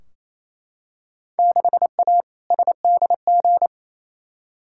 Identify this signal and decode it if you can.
Morse code, heard as a keyed tone, about 27 wpm, 700 Hz. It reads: 6A SDG
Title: Field Day Exchanges Encore - 28wpm